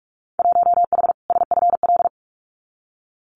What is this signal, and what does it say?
Morse code, heard as a keyed tone, about 46 wpm, 700 Hz. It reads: 1H SFL
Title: Winter Field Day Exchanges 45wpm